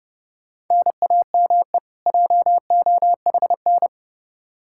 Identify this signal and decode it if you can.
Morse code, heard as a keyed tone, about 32 wpm, 700 Hz. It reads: NAME JOHN
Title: Practice QSO - Exchange 1 - Beginning - 30wpm